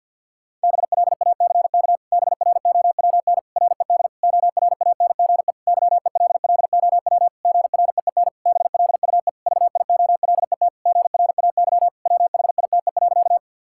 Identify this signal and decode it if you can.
Morse code, heard as a keyed tone, about 50 wpm, 700 Hz. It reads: BLACK BROWN RED ORANGE YELLOW GREEN BLUE VIOLET GRAY WHITE.
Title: Pangram Challenge v2 - 50wpm